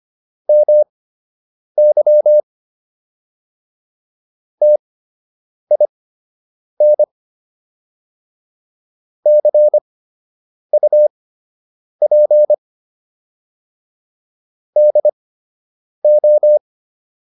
Morse code, heard as a keyed tone, about 25 wpm, spaced as for 8 wpm, 600 Hz. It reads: MY TIN CUP DO